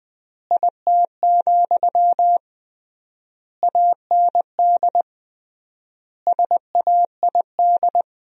Morse code, heard as a keyed tone, about 20 wpm, 700 Hz. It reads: IT, AND SAID
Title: 3x Word-Spacing - Sentences from Top 500 Words - 20wpm